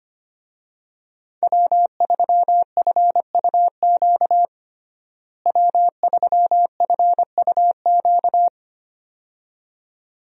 Morse code, heard as a keyed tone, about 25 wpm, 700 Hz. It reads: W3FUQ W3FUQ
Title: USA Call Signs - 5 Characters - Repeated 2x - 25wpm